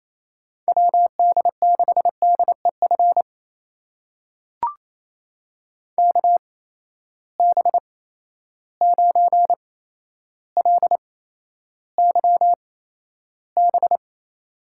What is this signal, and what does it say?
Morse code, heard as a keyed tone, about 28 wpm, spaced as for 12 wpm, 700 Hz. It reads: WD6DEF KB9LYB
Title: USA Call Signs - 6 Characters - 8x Character Spacing - 28wpm